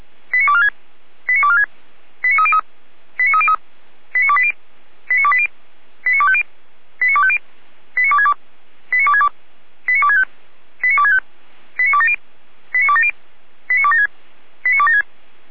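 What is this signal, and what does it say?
Five-tone selective call (ZVEI), heard as quick runs of five tones, tones of about 70 ms. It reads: 89246 89246 89303 89303 89280 89280 89250 89250 89262 89262 89265 89265 89270 89270 89276 89276
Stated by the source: A ZVEI five-tone selective call, code 89169, received with an SDR.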